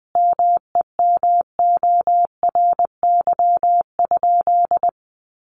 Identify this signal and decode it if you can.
Morse code, heard as a keyed tone, about 20 wpm, 700 Hz. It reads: MEMORY?
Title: Sentences from Top 1300 Words 20wpm